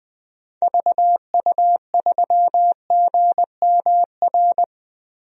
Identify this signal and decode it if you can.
Morse code, heard as a keyed tone, about 20 wpm, 700 Hz. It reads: VU3GMR
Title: DXCC-100 Call Signs - 6 Characters - Repeated 2x - 20wpm